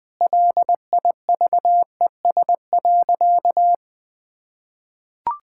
Morse code, heard as a keyed tone, about 20 wpm, 700 Hz. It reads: LIVES.
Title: Sentences with 2-Word Phrases - 20wpm